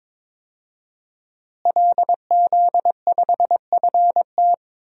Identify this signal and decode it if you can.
Morse code, heard as a keyed tone, about 22 wpm, 700 Hz. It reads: LZ5FT
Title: DXCC-100 Call Signs - 5 Characters - Repeated 2x - 22wpm